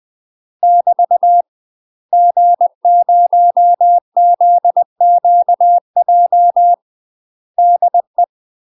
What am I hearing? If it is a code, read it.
Morse code, heard as a keyed tone, about 20 wpm, 700 Hz. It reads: <BT> G0ZQJ DE
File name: UK MASTER QSOs TRAINING / 20WPM / UK MASTER QSOs 0028